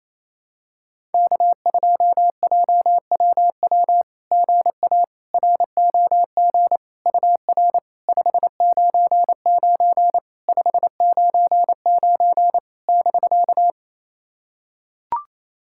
Morse code, heard as a keyed tone, about 28 wpm, 690 Hz. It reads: K2JWW GA ROG UR 599 599 <BK>
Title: SOTA QSO Practice 28wpm